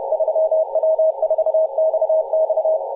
Morse code, heard as a keyed tone, about 30 wpm, 640 Hz, with radio noise. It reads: 3W4XX
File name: posts / 2016-aug-17-0039z-3w4xx-14-022mhz-cw